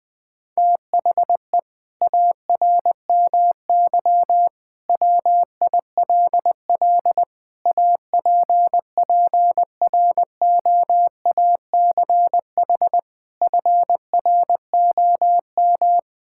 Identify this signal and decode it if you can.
Morse code, heard as a keyed tone, about 20 wpm, 700 Hz. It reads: THE ARMY WILL APPROACH FROM
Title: Sentences from Top 1000 Words - Review Part 6 - 20wpm